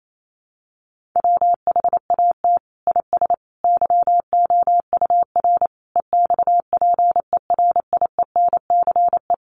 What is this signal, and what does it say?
Morse code, heard as a keyed tone, about 28 wpm, 700 Hz. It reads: WHAT IS YOUR EXPERIENCE
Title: Sentences from Top 600 Words 28wpm